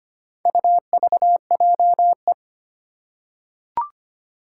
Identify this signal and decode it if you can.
Morse code, heard as a keyed tone, about 25 wpm, 700 Hz. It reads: UVJE
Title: Four Challenging Letter-Number Combinations 25wpm